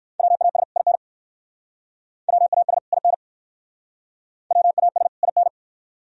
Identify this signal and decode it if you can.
Morse code, heard as a keyed tone, about 58 wpm, 690 Hz. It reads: WAS IN WAS IN WAS IN